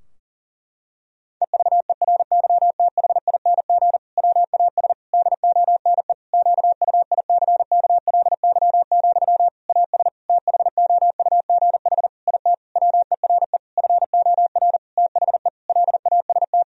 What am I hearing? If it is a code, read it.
Morse code, heard as a keyed tone, about 40 wpm, 700 Hz. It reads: EVERYTHING WAS DONE QUICKLY, AS THOUGH IT WERE FOR THE LAST
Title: Sentences from Top 1000 Words - Review Part 2 - 40wpm